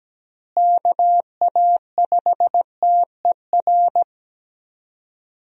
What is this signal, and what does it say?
Morse code, heard as a keyed tone, about 17 wpm, 700 Hz. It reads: KA5TER
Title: USA Call Signs - 6 Characters - Repeated 3x - 17wpm